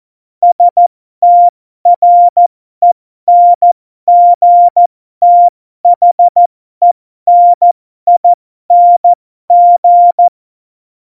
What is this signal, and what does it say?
Morse code, heard as a keyed tone, about 14 wpm, spaced as for 12 wpm, 700 Hz. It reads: STRENGTHENING
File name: Morsekurs / Witze